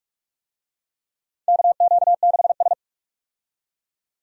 Morse code, heard as a keyed tone, about 44 wpm, 700 Hz. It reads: KQ6S
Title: USA Call Signs - 4 Characters - 45wpm